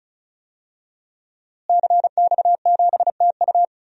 Morse code, heard as a keyed tone, about 34 wpm, 700 Hz. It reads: CX7TU
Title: DXCC-100 Call Signs - 5 Characters - Repeated 3x - 35wpm